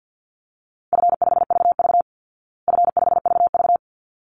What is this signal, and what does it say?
Morse code, heard as a keyed tone, about 50 wpm, 700 Hz. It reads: F5VV F5VV
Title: Easy Call Signs Repeated 2x - 50wpm